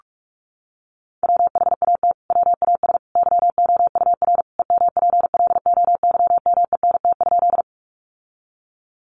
Morse code, heard as a keyed tone, about 44 wpm, 700 Hz. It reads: WHAT WAS YOUR EMPLOYMENT?
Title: Sentences from Top 1500 Words 45wpm